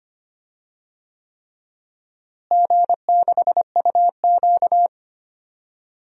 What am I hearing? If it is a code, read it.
Morse code, heard as a keyed tone, about 25 wpm, 700 Hz. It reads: G6UQ
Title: DXCC-100 Call Signs - 4 Characters - Repeated 3x - 25wpm